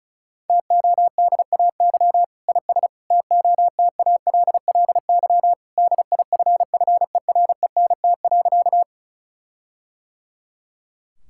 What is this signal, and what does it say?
Morse code, heard as a keyed tone, about 36 wpm, 700 Hz. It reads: TODAY IS TOTALLY DIFFERENT.
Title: Sentences from Top 1600 Words 35wpm